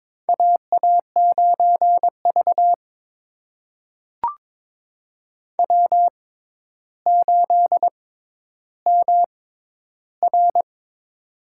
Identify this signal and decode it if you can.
Morse code, heard as a keyed tone, about 22 wpm, spaced as for 11 wpm, 700 Hz. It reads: AA9V W8MR
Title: USA Call Signs - 4 Characters - 6x Character Spacing - 22wpm